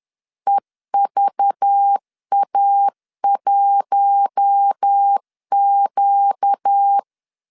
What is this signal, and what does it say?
Morse code, heard as a keyed tone, about 10 wpm, 800 Hz. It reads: EVA1Q
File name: LEZIONI / 6 PKX0 / 6 P 10wpm 2432